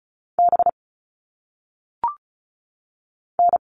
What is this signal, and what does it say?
Morse code, heard as a keyed tone, about 36 wpm, 700 Hz. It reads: B N
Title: All Cut Numbers - 1-Digit 35wpm